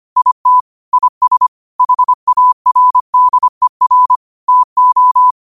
Morse code, heard as a keyed tone, about 25 wpm, 1000 Hz. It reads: IT IS HARDER TO